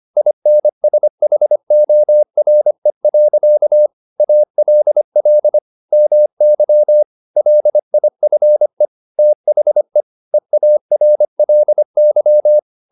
Morse code, heard as a keyed tone, about 25 wpm, 600 Hz. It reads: INSHORE. ALL MY LIFE THE EARLY
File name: the old man and the sea part 1 25wpm